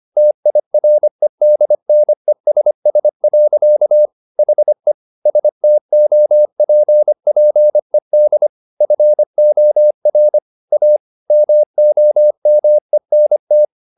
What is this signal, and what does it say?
Morse code, heard as a keyed tone, about 25 wpm, 600 Hz. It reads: TIREDNESS. HE STOPPED FOR A MOMENT